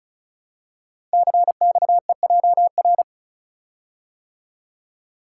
Morse code, heard as a keyed tone, about 36 wpm, 700 Hz. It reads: CXEJR
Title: Code Groups - 5 Letter:Numbers - 35wpm